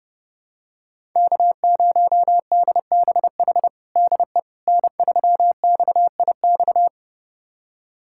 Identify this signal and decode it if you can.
Morse code, heard as a keyed tone, about 30 wpm, 700 Hz. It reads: K0DBH DE N3XIX
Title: USA Callsign DE Callsign - 5 Characters - 30wpm